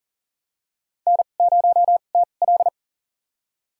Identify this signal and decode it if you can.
Morse code, heard as a keyed tone, about 40 wpm, spaced as for 29 wpm, 700 Hz. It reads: N0TL
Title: USA Call Signs - 4 Characters - 2x Character Spacing - 40wpm